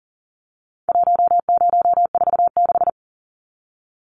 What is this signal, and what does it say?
Morse code, heard as a keyed tone, about 40 wpm, 700 Hz. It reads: 1046
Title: Numbers - 4-Digit 40wpm